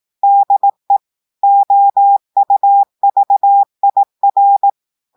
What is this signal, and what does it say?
Morse code, heard as a keyed tone, about 18 wpm, 800 Hz. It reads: DE OUVIR
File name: audios / portugais / portugais texte en clair 2 vitesse 18 mots minutes